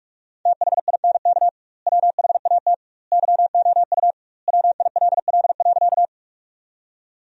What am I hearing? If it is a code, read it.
Morse code, heard as a keyed tone, about 43 wpm, 700 Hz. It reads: THINK WHAT YOU WILL.
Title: 2x Word-Spacing - Sentences from Top 200 Words - 45wpm